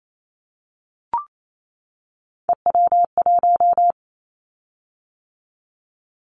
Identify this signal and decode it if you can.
Morse code, heard as a keyed tone, about 28 wpm, 700 Hz. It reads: EW1
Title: Callsign Prefixes for Top 100 DXCC - 28wpm